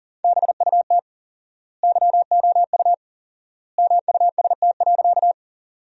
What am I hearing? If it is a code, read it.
Morse code, heard as a keyed tone, about 40 wpm, 700 Hz. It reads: BUT YOU MUST.
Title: 4x Word-Spacing - Sentences from Top 200 Words - 40wpm